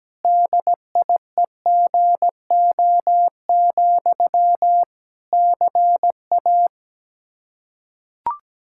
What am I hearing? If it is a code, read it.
Morse code, heard as a keyed tone, about 17 wpm, 700 Hz. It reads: DIEGO, CA
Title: Practice QSO - Exchange 1 - Beginning - 17wpm